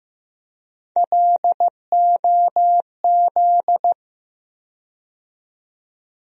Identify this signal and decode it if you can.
Morse code, heard as a keyed tone, about 15 wpm, 700 Hz. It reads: LOZ
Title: USA Call Sign Postfixes - 3 Letters 15wpm